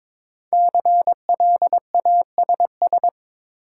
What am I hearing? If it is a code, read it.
Morse code, heard as a keyed tone, about 22 wpm, 700 Hz. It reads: CLASS
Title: Sentences from Top 1000 Words - Review Part 2 - New Word Drill - 22wpm